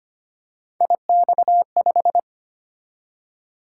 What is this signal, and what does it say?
Morse code, heard as a keyed tone, about 25 wpm, 700 Hz. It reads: IX5
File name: Three Letter-Number Combinations - 25wpm